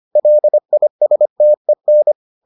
Morse code, heard as a keyed tone, about 25 wpm, 600 Hz. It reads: LISTEN